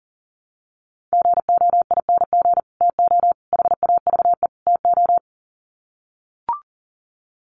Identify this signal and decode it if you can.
Morse code, heard as a keyed tone, about 40 wpm, 700 Hz. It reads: GOING TO HAVE TO
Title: Sets of 4 Words 40wpm